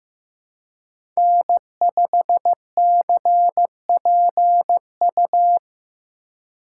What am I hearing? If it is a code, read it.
Morse code, heard as a keyed tone, about 15 wpm, 700 Hz. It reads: N5CPU
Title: USA Call Signs - 5 Characters - 2x Character Spacing - 15wpm